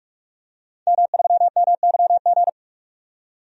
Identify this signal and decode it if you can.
Morse code, heard as a keyed tone, about 43 wpm, 700 Hz. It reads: M3MYG
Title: DXCC-100 Call Signs - 5 Characters - 45wpm